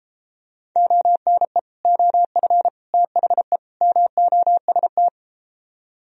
Morse code, heard as a keyed tone, about 32 wpm, 700 Hz. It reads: ONE OF THE MOST